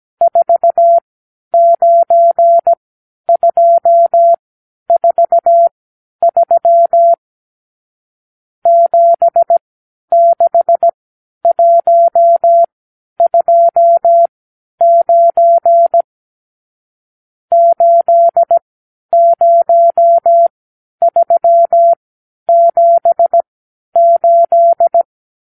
Morse code, heard as a keyed tone, about 17 wpm, spaced as for 10 wpm, 680 Hz. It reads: 49243 76129 80378
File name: arrl / Review (1-0)